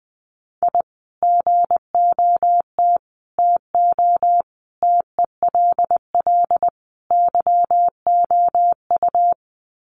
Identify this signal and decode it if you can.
Morse code, heard as a keyed tone, about 20 wpm, 700 Hz. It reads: I GOT TO TELL YOU
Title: Sets of 5 Words Encore2 20wpm